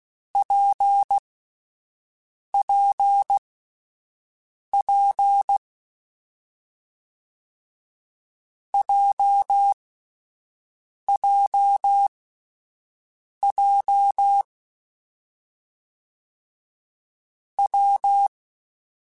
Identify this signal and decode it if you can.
Morse code, heard as a keyed tone, about 16 wpm, spaced as for 5 wpm, 2950 Hz. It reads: PPP JJJ W
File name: Difficult Combos / Difficult Combos-mp3s / WPJ WPJ1 @10